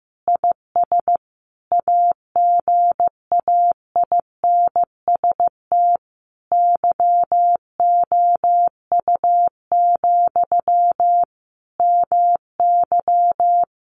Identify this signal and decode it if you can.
Morse code, heard as a keyed tone, about 15 wpm, 700 Hz. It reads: IS AGAINST YOU, MY
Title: Sentences from Top 700 Words 15wpm